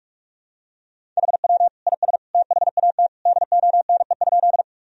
Morse code, heard as a keyed tone, about 46 wpm, 700 Hz. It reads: HW IS THAT DONE?
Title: Sentences from Top 500 Words and Abbreviations 45wpm